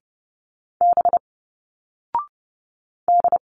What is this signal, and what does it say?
Morse code, heard as a keyed tone, about 30 wpm, 700 Hz. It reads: B D
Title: Single Letters 30wpm